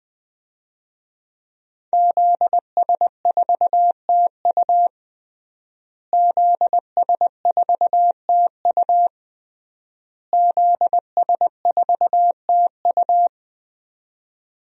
Morse code, heard as a keyed tone, about 20 wpm, 700 Hz. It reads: ZS4TU ZS4TU ZS4TU
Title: DXCC-100 Call Signs - 5 Characters - Repeated 3x - 20wpm